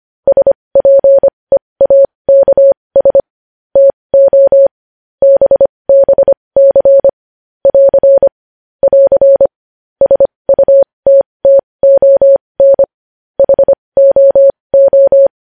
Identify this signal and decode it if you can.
Morse code, heard as a keyed tone, about 25 wpm, spaced as for 20 wpm, 550 Hz. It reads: SPEAKS TO BBC <AR> <AR> SUTTON HOO
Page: news-headlines-in-morse-code-at-20-wpm